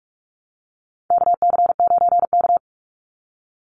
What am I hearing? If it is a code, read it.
Morse code, heard as a keyed tone, about 44 wpm, 700 Hz. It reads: KC9K
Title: Call Signs Repeated 2x - 45wpm